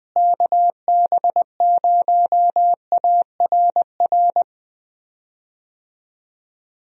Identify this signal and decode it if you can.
Morse code, heard as a keyed tone, about 20 wpm, 700 Hz. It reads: KB0ARR
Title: USA Call Signs - 6 Characters - Repeated 3x - 20wpm